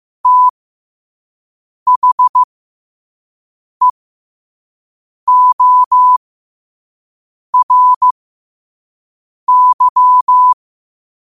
Morse code, heard as a keyed tone, about 15 wpm, spaced as for 5 wpm, 1000 Hz. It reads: THEORY
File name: qotdfiles / 5wpm